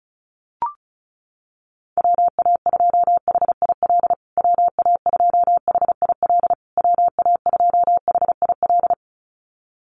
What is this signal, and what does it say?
Morse code, heard as a keyed tone, about 34 wpm, 700 Hz. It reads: WA2HIL WA2HIL WA2HIL
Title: USA Call Signs Repeated 3x 35wpm